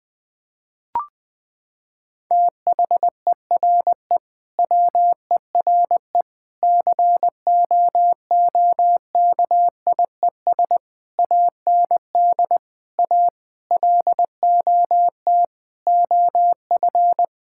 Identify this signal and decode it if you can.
Morse code, heard as a keyed tone, about 21 wpm, 700 Hz. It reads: THERE WERE COOKIES AND A LOT OF
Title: Sentences with 4-Word Phrases - 20wpm